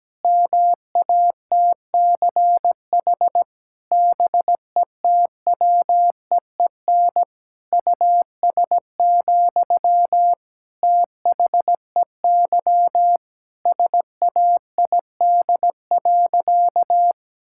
Morse code, heard as a keyed tone, about 17 wpm, 700 Hz. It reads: MATCH BETWEEN US, THEY SAID.
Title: Sentences from Top 1600 Words 17wpm